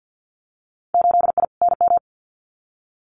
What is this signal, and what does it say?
Morse code, heard as a keyed tone, about 50 wpm, 700 Hz. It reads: 8I NM
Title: Winter Field Day Exchanges Encore 50wpm